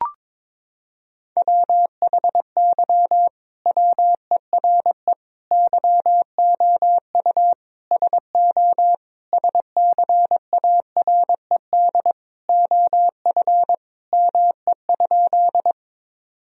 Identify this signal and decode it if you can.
Morse code, heard as a keyed tone, about 22 wpm, 700 Hz. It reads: WHY WERE YOU SO SCARED OF ME?
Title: Sentences from Top 1000 Words - Review Part 4 - 22wpm